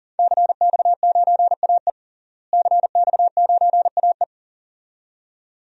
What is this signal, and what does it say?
Morse code, heard as a keyed tone, about 40 wpm, 700 Hz. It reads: CX9AE CX9AE
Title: DXCC-100 Call Signs - 5 Characters - Repeated 2x - 40wpm